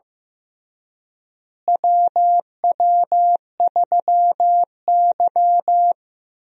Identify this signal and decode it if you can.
Morse code, heard as a keyed tone, about 15 wpm, 700 Hz. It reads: WW3Y
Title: USA Call Signs - 4 Characters - Repeated 3x - 15wpm